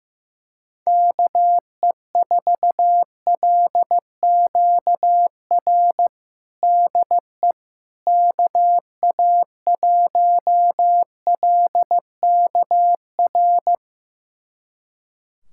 Morse code, heard as a keyed tone, about 15 wpm, 700 Hz. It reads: KE4LQR DE KA1LKR
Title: USA Callsign DE Callsign - 6 Characters - 15wpm